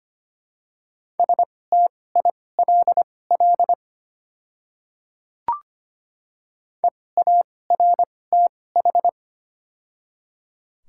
Morse code, heard as a keyed tone, about 25 wpm, spaced as for 18 wpm, 700 Hz. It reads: STILL EARTH
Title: 2x Character Spacing - Top 500 Words - 25wpm